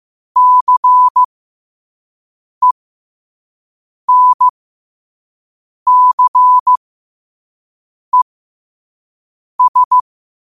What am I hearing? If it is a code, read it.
Morse code, heard as a keyed tone, about 15 wpm, spaced as for 5 wpm, 1000 Hz. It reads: CENCES
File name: qotdfiles / 5wpm